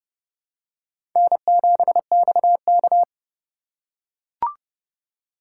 Morse code, heard as a keyed tone, about 30 wpm, 700 Hz. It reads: N7XK
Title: USA Call Signs - 4 Characters - 2x Character Spacing - 30wpm